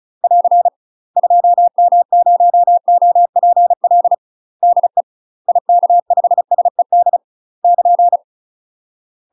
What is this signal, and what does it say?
Morse code, heard as a keyed tone, about 35 wpm, 700 Hz. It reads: <AR> 2M0OPL DE IK5SED <KN>
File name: UK MASTER QSOs TRAINING / 35WPM / UK MASTER QSOs 0031